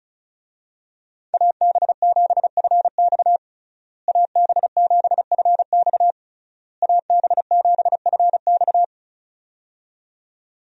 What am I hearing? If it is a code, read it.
Morse code, heard as a keyed tone, about 36 wpm, 700 Hz. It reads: AB7FX AB7FX AB7FX
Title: USA Call Signs - 5 Characters - Repeated 3x - 35wpm